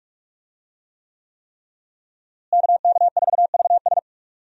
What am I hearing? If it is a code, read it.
Morse code, heard as a keyed tone, about 46 wpm, 700 Hz. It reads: KK4VS
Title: USA Call Signs - 5 Characters - Repeated 3x - 45wpm